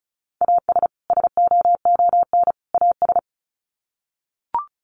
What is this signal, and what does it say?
Morse code, heard as a keyed tone, about 34 wpm, 700 Hz. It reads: AS SOON AS
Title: Sets of 3 Words 35wpm